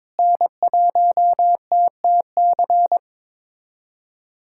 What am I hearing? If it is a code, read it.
Morse code, heard as a keyed tone, about 22 wpm, 700 Hz. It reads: N1TTC
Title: USA Callsign DE Callsign - 5 Characters - 22wpm